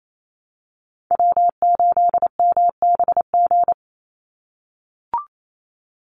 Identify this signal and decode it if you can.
Morse code, heard as a keyed tone, about 28 wpm, 700 Hz. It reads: W8MBG
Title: USA Call Signs 28wpm